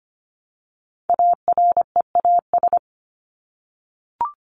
Morse code, heard as a keyed tone, about 25 wpm, 700 Hz. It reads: AREAS
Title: ES:S Suffix - Top 100 Words with Base Words 25wpm